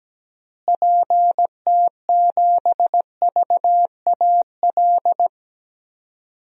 Morse code, heard as a keyed tone, about 17 wpm, 700 Hz. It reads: PT7VAL
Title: DXCC-100 Call Signs - 6 Characters - Repeated 3x - 17wpm